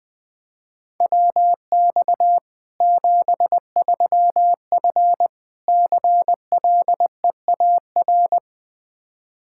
Morse code, heard as a keyed tone, about 21 wpm, 700 Hz. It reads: WX 73F CLEAR
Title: Practice QSO - Exchange 2 - Info - 20wpm